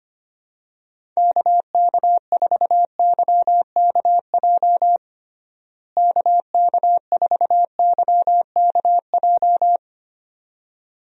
Morse code, heard as a keyed tone, about 25 wpm, 700 Hz. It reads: KK4YKJ KK4YKJ
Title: USA Call Signs - 6 Characters - Repeated 2x - 25wpm